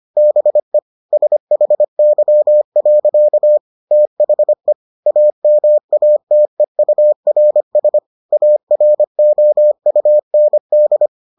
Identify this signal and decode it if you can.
Morse code, heard as a keyed tone, about 25 wpm, 600 Hz. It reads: BE SHY. THE AMATEURS AROUND